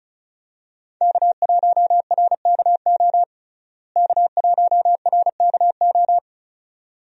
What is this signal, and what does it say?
Morse code, heard as a keyed tone, about 36 wpm, 700 Hz. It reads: K1RKO K1RKO
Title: USA Call Signs - 5 Characters - Repeated 2x - 35wpm